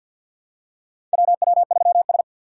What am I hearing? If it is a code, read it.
Morse code, heard as a keyed tone, about 53 wpm, 700 Hz. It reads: WW3S
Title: USA Call Signs - 4 Characters - 6x Character Spacing - 50wpm